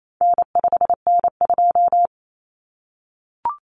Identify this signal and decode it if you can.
Morse code, heard as a keyed tone, about 28 wpm, 700 Hz. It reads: N5N2
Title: Common Cut Numbers - 4-Digit 28wpm